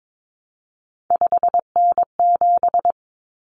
Morse code, heard as a keyed tone, about 22 wpm, 700 Hz. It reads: 5N7
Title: Common Cut Numbers - 3-Digit 22wpm